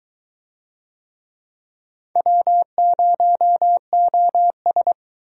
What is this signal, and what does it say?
Morse code, heard as a keyed tone, about 19 wpm, 700 Hz. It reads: W0OS